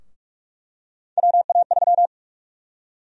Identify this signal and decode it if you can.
Morse code, heard as a keyed tone, about 46 wpm, 700 Hz. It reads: WA3
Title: USA Call Sign Prefixes - 45wpm